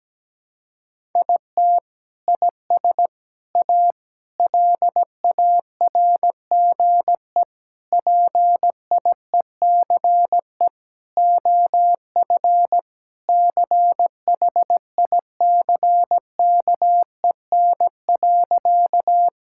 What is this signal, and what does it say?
Morse code, heard as a keyed tone, about 17 wpm, 700 Hz. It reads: IT IS A LARGE PIECE OF CHICKEN.
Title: Sentences from Top 1000 Words - Review Part 5 - 17wpm